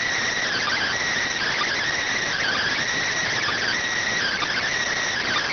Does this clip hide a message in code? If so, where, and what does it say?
Yes: Emergency Alert System (SAME) tones (header bursts) from 0.4 to 4.7 s, heard as harsh buzzing bursts.